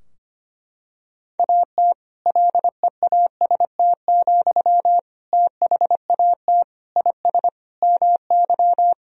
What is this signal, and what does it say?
Morse code, heard as a keyed tone, about 25 wpm, 700 Hz. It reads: AT LEAST, THAT IS MY
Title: Sentences from Top 1000 Words - Review Part 3 - 25wpm